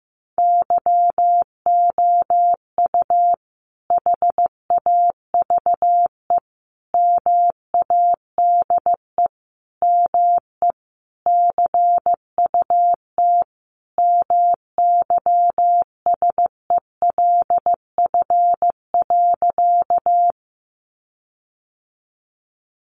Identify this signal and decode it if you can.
Morse code, heard as a keyed tone, about 15 wpm, 700 Hz. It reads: YOU HAVE MADE ME CUT MYSELF.
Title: Sentences from Top 1300 Words 15wpm